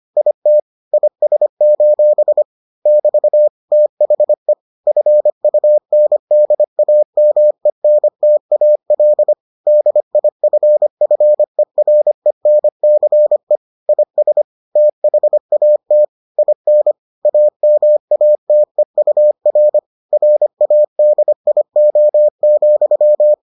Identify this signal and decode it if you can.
Morse code, heard as a keyed tone, about 25 wpm, 600 Hz. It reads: IT IS: <BT> THE FUNDAMENTAL DIFFERENCE IS THAT IN AMATEUR RADIO,